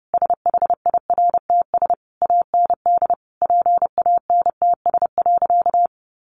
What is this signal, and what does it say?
Morse code, heard as a keyed tone, about 30 wpm, 700 Hz. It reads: SHIRTS AND PANTS.
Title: A Sound Of Thunder 30wpm